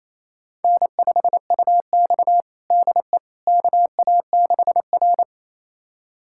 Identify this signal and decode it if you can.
Morse code, heard as a keyed tone, about 28 wpm, 700 Hz. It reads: N5UX DE KA6R
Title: USA Callsign DE Callsign - 4 Characters - 28wpm